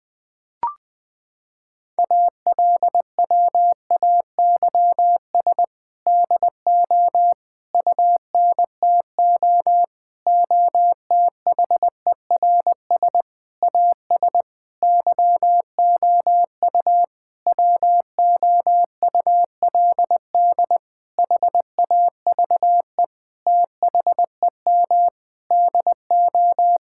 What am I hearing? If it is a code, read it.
Morse code, heard as a keyed tone, about 20 wpm, 700 Hz. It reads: ALWAYS DO UNTO OTHERS AS YOU WOULD HAVE THEM DO
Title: Sentences with 8-Word Phrases - 20wpm